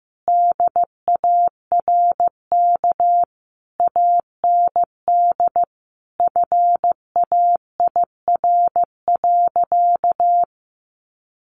Morse code, heard as a keyed tone, about 15 wpm, 700 Hz. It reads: DARK AND FAIR.
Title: Sentences from Top 1300 Words 15wpm